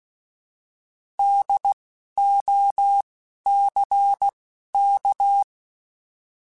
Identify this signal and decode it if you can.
Morse code, heard as a keyed tone, about 16 wpm, 800 Hz. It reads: D O C K
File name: Alphabetical / Alphabetical mp3s / Ds @10